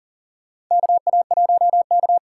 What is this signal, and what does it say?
Morse code, heard as a keyed tone, about 39 wpm, 700 Hz. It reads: KA1K
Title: USA Call Signs - 6 Characters - 8x Character Spacing - 40wpm